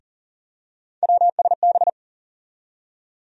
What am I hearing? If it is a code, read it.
Morse code, heard as a keyed tone, about 40 wpm, 700 Hz. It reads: WSB
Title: USA Call Sign Postfixes - 3 Letters 40wpm